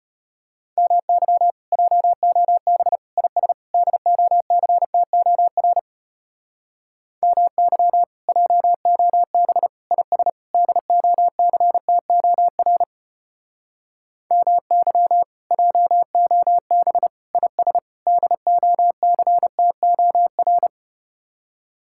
Morse code, heard as a keyed tone, about 33 wpm, 700 Hz. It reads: MY JOB IS DOCTOR MY JOB IS DOCTOR MY JOB IS DOCTOR